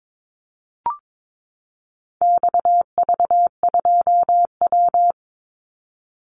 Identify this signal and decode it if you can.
Morse code, heard as a keyed tone, about 22 wpm, 700 Hz. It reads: XV2W
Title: Call Signs Repeated 3x - 22wpm